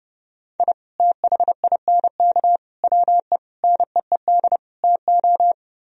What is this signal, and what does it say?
Morse code, heard as a keyed tone, about 30 wpm, 700 Hz. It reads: I THINK WE NEED TO